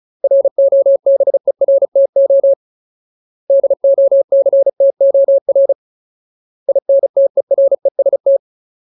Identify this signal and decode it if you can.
Morse code, heard as a keyed tone, about 34 wpm, 550 Hz. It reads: ROBERTO DOCTOR INTEREST
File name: Ultimate Word List / 35WPM Ultimate